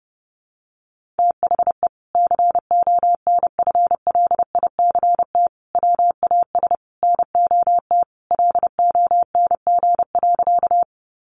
Morse code, heard as a keyed tone, about 30 wpm, 700 Hz. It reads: THE CONFLICT WAS NOT LONG.
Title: Sentences from Top 1300 Words 30wpm